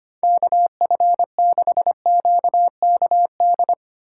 Morse code, heard as a keyed tone, about 25 wpm, 700 Hz. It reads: KF6QKD
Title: USA Call Signs - 6 Characters - 6x Character Spacing - 25wpm